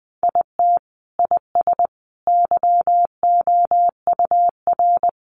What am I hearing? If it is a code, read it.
Morse code, heard as a keyed tone, about 20 wpm, 700 Hz. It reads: IT IS YOUR
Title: Sentences from Top 800 Words 20wpm